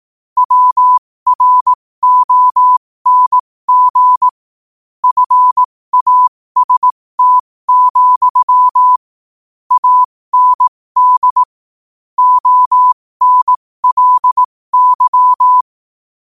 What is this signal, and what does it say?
Morse code, heard as a keyed tone, about 18 wpm, spaced as for 15 wpm, 1000 Hz. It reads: WRONG FAST, AND ONLY
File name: qotdfiles / 15wpm